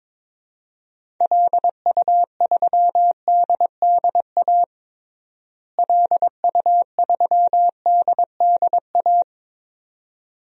Morse code, heard as a keyed tone, about 22 wpm, 700 Hz. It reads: LU3DDA LU3DDA
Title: DXCC-100 Call Signs - 6 Characters - Repeated 2x - 22wpm